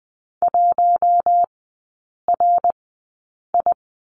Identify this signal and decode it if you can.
Morse code, heard as a keyed tone, about 20 wpm, spaced as for 8 wpm, 690 Hz. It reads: 1RI
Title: 060 Lesson 11e - Letter h - Three Characters 20wpm